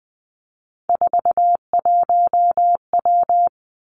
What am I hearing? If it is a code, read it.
Morse code, heard as a keyed tone, about 20 wpm, 700 Hz. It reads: 41W
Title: Three Challenging Letter-Number Combinations 20wpm